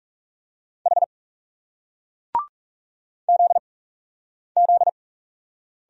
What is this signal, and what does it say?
Morse code, heard as a keyed tone, about 40 wpm, 700 Hz. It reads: H Z Z